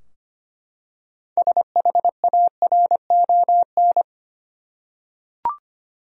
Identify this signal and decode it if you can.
Morse code, heard as a keyed tone, about 25 wpm, 700 Hz. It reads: SHARON
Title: USA Common Names - 25wpm